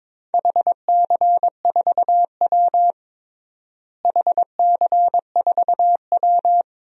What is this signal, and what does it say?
Morse code, heard as a keyed tone, about 22 wpm, 700 Hz. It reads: HC4W HC4W
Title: DXCC-100 Call Signs - 4 Characters - Repeated 3x - 22wpm